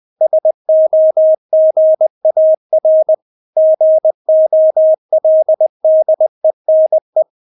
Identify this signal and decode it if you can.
Morse code, heard as a keyed tone, about 20 wpm, 630 Hz. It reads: SOGAR GOLDENE
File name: Witz1 20wpm